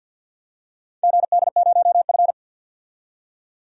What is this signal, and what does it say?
Morse code, heard as a keyed tone, about 51 wpm, 700 Hz. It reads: GD0F
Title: DXCC-100 Call Signs - 4 Characters - 50wpm